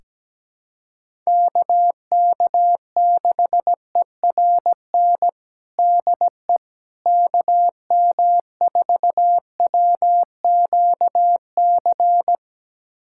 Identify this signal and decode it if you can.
Morse code, heard as a keyed tone, about 17 wpm, 700 Hz. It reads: KK6ERN DE KM4WQC
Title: USA Callsign DE Callsign - 6 Characters - 17wpm